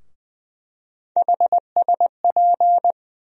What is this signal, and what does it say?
Morse code, heard as a keyed tone, about 20 wpm, 700 Hz. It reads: HSP
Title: USA Call Sign Postfixes - 3 Letters 20wpm